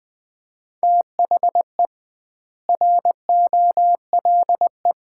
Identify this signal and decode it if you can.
Morse code, heard as a keyed tone, about 20 wpm, 700 Hz. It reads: THE ROLE
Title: 2x Word-Spacing - Sets of 4 Words - 20wpm